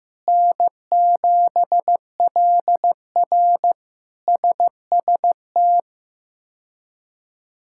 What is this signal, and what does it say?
Morse code, heard as a keyed tone, about 15 wpm, 700 Hz. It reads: N7LR SST
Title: K1USN SST QSO Practice - 15wpm